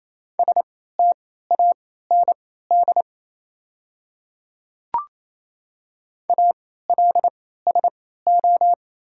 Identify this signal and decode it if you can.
Morse code, heard as a keyed tone, about 28 wpm, spaced as for 16 wpm, 690 Hz. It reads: STAND ALSO
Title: 3x Character Spacing - Top 300 Words - 28wpm